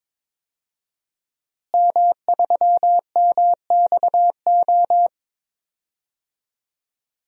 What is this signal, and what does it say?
Morse code, heard as a keyed tone, about 22 wpm, 700 Hz. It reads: M3MXO
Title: DXCC-100 Call Signs - 5 Characters - Repeated 2x - 22wpm